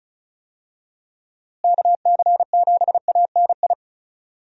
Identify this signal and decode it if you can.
Morse code, heard as a keyed tone, about 36 wpm, 700 Hz. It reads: KC7ANI
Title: USA Call Signs - 6 Characters - Repeated 3x - 35wpm